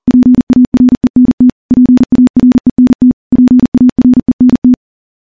Morse code, heard as a keyed tone, about 40 wpm, 250 Hz. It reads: PARENT PARENT PARENT
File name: morse-training-40WPM